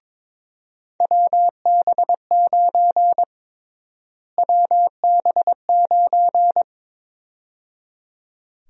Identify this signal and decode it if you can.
Morse code, heard as a keyed tone, about 22 wpm, 700 Hz. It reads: WB9 WB9
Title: USA Call Sign Prefixes - Repeated 2x - 22wpm